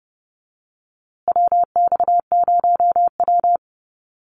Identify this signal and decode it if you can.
Morse code, heard as a keyed tone, about 30 wpm, 700 Hz. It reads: WX0W
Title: USA Call Signs Repeated 2x 30wpm